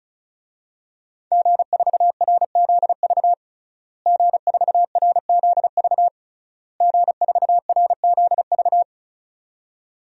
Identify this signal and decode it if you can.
Morse code, heard as a keyed tone, about 35 wpm, 700 Hz. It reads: G4RZV G4RZV G4RZV
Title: DXCC-100 Call Signs - 5 Characters - Repeated 3x - 35wpm